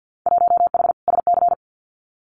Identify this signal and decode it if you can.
Morse code, heard as a keyed tone, about 50 wpm, 700 Hz. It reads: 1H SC
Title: Winter Field Day Exchanges 50wpm